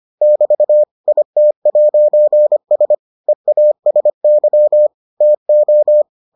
Morse code, heard as a keyed tone, about 25 wpm, 600 Hz. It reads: <BT> IT'S EASY TO